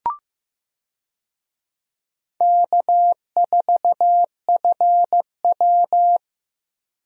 Morse code, heard as a keyed tone, about 15 wpm, 700 Hz. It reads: K4FW
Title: USA Call Signs - 4 Characters - Repeated 2x - 15wpm